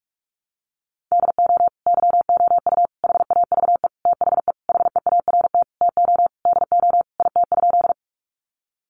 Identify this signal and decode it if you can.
Morse code, heard as a keyed tone, about 44 wpm, 700 Hz. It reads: DO YOU HAVE THE HEART TO DO IT?
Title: Sentences from Top 1200 Words 45wpm